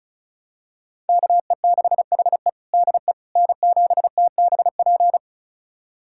Morse code, heard as a keyed tone, about 34 wpm, 700 Hz. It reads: KE6HE DE N7TBP
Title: USA Callsign DE Callsign - 5 Characters - 35wpm